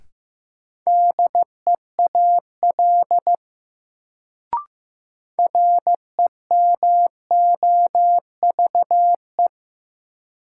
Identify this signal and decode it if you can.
Morse code, heard as a keyed tone, about 15 wpm, 700 Hz. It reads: DEAL REMOVE
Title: Sentences from Top 800 Words - New Word Drill - 15wpm